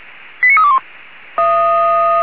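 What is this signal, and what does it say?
Five-tone selective call (ZVEI), heard as quick runs of five tones, tones of about 70 ms. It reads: 89321
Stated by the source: ZVEI five-tone call received on SDR.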